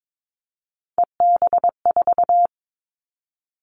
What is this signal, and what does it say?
Morse code, heard as a keyed tone, about 22 wpm, 700 Hz. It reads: EB4
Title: Callsign Prefixes for Top 25 DXCC - Repeated 3x - 22wpm